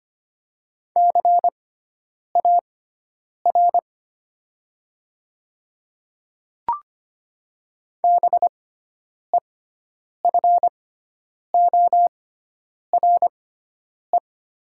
Morse code, heard as a keyed tone, about 25 wpm, spaced as for 9 wpm, 700 Hz. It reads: CAR BEFORE
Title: 6x Character Spacing - Top 300 Words - 25wpm